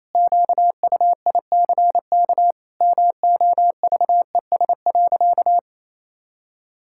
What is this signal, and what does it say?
Morse code, heard as a keyed tone, about 28 wpm, 700 Hz. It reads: QUICK MOVES.
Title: Pangram Challenge v2 - 28wpm